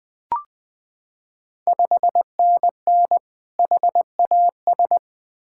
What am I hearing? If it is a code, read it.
Morse code, heard as a keyed tone, about 20 wpm, 700 Hz. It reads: 5NN HAS
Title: Ontario QSO Party Exchange 20wpm